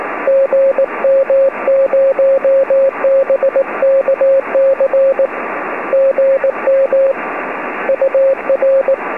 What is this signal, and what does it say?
Morse code, heard as a keyed tone, about 18 wpm, 540 Hz, with radio noise. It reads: GM0BKC GM UR